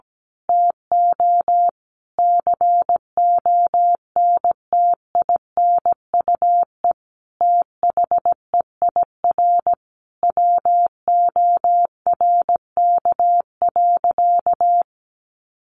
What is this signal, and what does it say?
Morse code, heard as a keyed tone, about 17 wpm, 700 Hz. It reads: TO CONTINUE THEIR WORK.
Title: Sentences from Top 1100 Words 17wpm